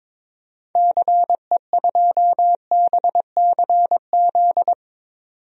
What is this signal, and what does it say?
Morse code, heard as a keyed tone, about 22 wpm, 700 Hz. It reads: CE2BCZ
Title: DXCC-100 Call Signs - 6 Characters - 22wpm